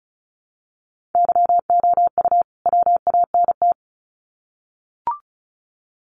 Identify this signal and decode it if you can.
Morse code, heard as a keyed tone, about 34 wpm, 700 Hz. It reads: YOU WANT
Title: Sets of 2 Words 35wpm